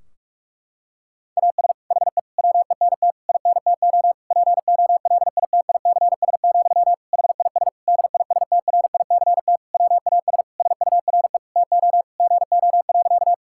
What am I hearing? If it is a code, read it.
Morse code, heard as a keyed tone, about 46 wpm, 700 Hz. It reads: AS HE WENT INTO POLITICS, HIS DISTRICT WAS SURE TO GO.
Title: Sentences from Top 1600 Words 45wpm